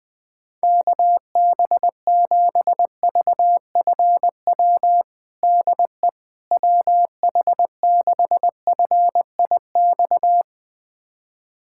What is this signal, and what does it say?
Morse code, heard as a keyed tone, about 20 wpm, 700 Hz. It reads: KB7VFW DE WH6FIX
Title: USA Callsign DE Callsign - 6 Characters - 20wpm